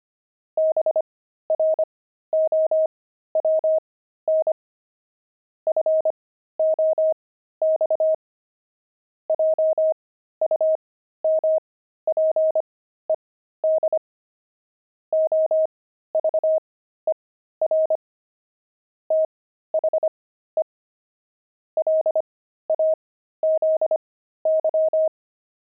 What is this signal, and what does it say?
Morse code, heard as a keyed tone, about 25 wpm, spaced as for 13 wpm, 630 Hz. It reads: BROWN FOX JUMPED OVER THE LAZY
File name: pangram-fs-13-cs-25-fox